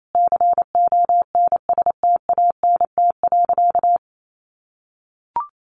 Morse code, heard as a keyed tone, about 28 wpm, 700 Hz. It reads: CONSTANT.
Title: Sentences from Top 1500 Words 28wpm